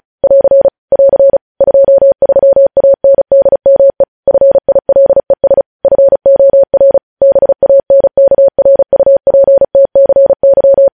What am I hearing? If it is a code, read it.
Morse code, heard as a keyed tone, about 36 wpm, 550 Hz. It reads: <AR> <AR> 23ANDME FILES FOR BANKRUPTCY
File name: podcast / News headlines-1x-35x35